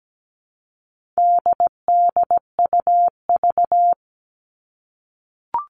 Morse code, heard as a keyed tone, about 17 wpm, 700 Hz. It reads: DDUV
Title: All Cut Numbers - 4-Digit 17wpm